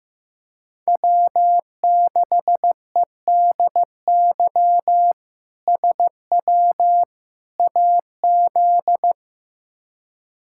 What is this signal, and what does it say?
Morse code, heard as a keyed tone, about 16 wpm, 700 Hz. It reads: W6EDY SW AZ
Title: K1USN SST QSO Practice - 15wpm